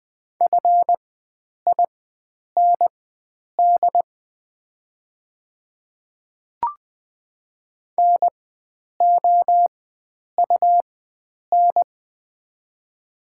Morse code, heard as a keyed tone, about 20 wpm, spaced as for 9 wpm, 700 Hz. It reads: FIND NOUN
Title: 4x Character Spacing - Top 500 Words - 20wpm